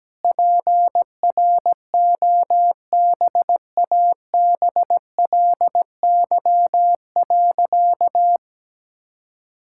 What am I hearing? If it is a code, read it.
Morse code, heard as a keyed tone, about 17 wpm, 700 Hz. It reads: PROBABLY.
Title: Sentences from Top 1000 Words - Review Part 5 - 17wpm